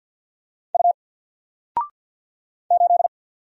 Morse code, heard as a keyed tone, about 48 wpm, 700 Hz. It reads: U 8
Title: Single Letter-Number 50wpm